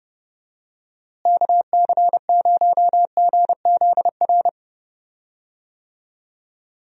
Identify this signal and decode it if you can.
Morse code, heard as a keyed tone, about 30 wpm, 700 Hz. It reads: KC0GZR
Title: USA Call Signs - 6 Characters - Repeated 2x - 30wpm